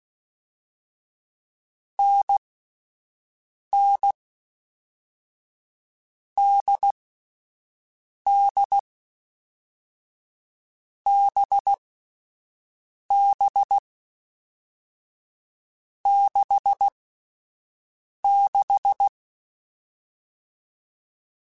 Morse code, heard as a keyed tone, about 16 wpm, 800 Hz. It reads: N N D D B B 6 6
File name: Difficult Combos / Difficult Combos-mp3s / NDB NDB6 @10